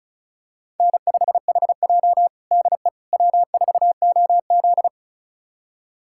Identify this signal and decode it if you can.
Morse code, heard as a keyed tone, about 36 wpm, 700 Hz. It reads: N5HJ DE W4OZ
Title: USA Callsign DE Callsign - 4 Characters - 35wpm